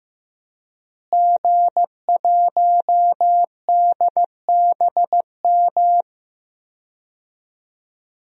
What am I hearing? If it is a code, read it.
Morse code, heard as a keyed tone, about 16 wpm, 700 Hz. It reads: G1DBM
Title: DXCC-100 Call Signs - 5 Characters - Repeated 3x - 15wpm